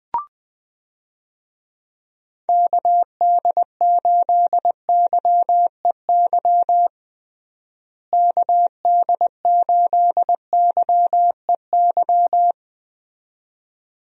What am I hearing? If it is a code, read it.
Morse code, heard as a keyed tone, about 20 wpm, 700 Hz. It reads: KD8YEY KD8YEY
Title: USA Call Signs - 6 Characters - Repeated 2x - 20wpm